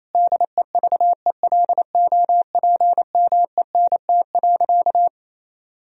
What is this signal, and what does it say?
Morse code, heard as a keyed tone, about 28 wpm, 700 Hz. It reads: DEVELOPMENT.
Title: Sentences from Top 1000 Words - Review Part 4 - 28wpm